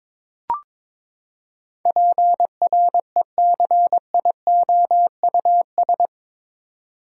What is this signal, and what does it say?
Morse code, heard as a keyed tone, about 22 wpm, 700 Hz. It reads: PRECIOUS
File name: practice / 22wpm / Eight Letter Words - 22wpm